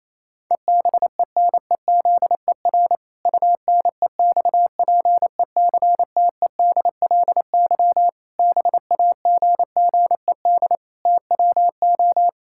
Morse code, heard as a keyed tone, about 28 wpm, 700 Hz. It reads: EBENEZER UNEXPECTEDLY BAGGED TWO
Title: Pangram Challenge v2 - 28wpm